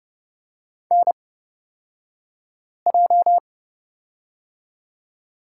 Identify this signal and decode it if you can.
Morse code, heard as a keyed tone, about 30 wpm, spaced as for 5 wpm, 700 Hz. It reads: NJ
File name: Lesson 36d - BT - Two Characters 30wpm